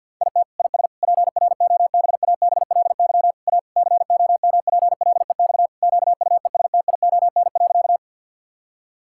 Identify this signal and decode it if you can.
Morse code, heard as a keyed tone, about 50 wpm, 700 Hz. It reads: IT IS PROBABLY A COMPLEX QUESTION.
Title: Sentences from Top 1000 Words - Review Part 6 - 50wpm